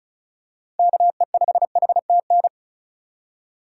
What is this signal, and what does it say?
Morse code, heard as a keyed tone, about 36 wpm, 700 Hz. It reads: KE5HTN
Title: USA Call Signs - 4x Character Spacing - 35wpm